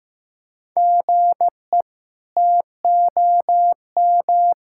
Morse code, heard as a keyed tone, about 16 wpm, 700 Hz. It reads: GE TOM
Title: K1USN SST QSO Practice - 15wpm